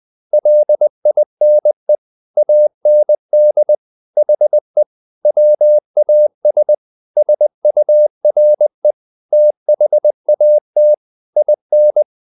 Morse code, heard as a keyed tone, about 20 wpm, 600 Hz. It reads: LINE AND HE WAS SURE THAT IN
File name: the old man and the sea part 2 20wpm